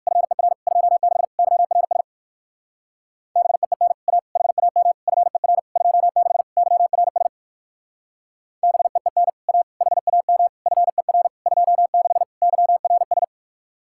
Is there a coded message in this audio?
Yes: Morse code from 0.1 to 13.2 s, heard as a keyed tone, about 60 wpm, 700 Hz. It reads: FER 26 YRS BEEN A HAM FER 26 YRS BEEN A HAM FER 26 YRS